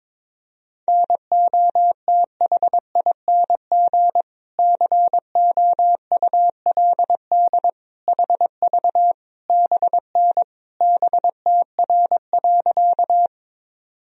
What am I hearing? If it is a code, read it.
Morse code, heard as a keyed tone, about 22 wpm, 700 Hz. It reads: NOTHING COULD HV BN BTR.
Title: Sentences from Top 500 Words and Abbreviations 22wpm